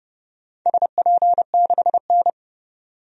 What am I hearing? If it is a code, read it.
Morse code, heard as a keyed tone, about 30 wpm, 700 Hz. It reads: SP6N
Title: DXCC-100 Call Signs - 4 Characters - 30wpm